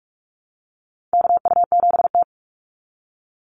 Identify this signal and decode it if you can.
Morse code, heard as a keyed tone, about 44 wpm, 700 Hz. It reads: KU7T
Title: Call Signs Repeated 3x - 45wpm